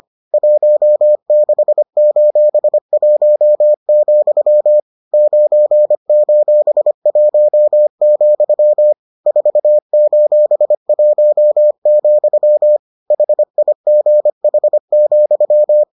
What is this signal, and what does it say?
Morse code, heard as a keyed tone, about 25 wpm, 600 Hz. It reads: 16:1, 9:1, 4:1, HIGH,